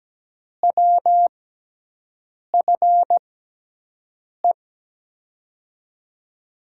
Morse code, heard as a keyed tone, about 17 wpm, spaced as for 6 wpm, 700 Hz. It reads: WFE
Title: USA Call Signs - 6 Characters - 6x Character Spacing - 17wpm